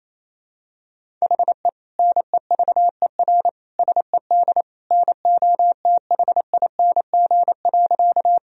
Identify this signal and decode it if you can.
Morse code, heard as a keyed tone, about 28 wpm, 700 Hz. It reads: HE NEVER SED NOTHING.
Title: Sentences from Top 500 Words and Abbreviations 28wpm